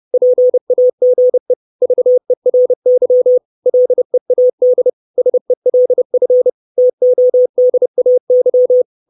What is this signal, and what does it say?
Morse code, heard as a keyed tone, about 30 wpm, 500 Hz. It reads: PAGE VERY LEAD SELF TODAY
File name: 632 Words 30WPM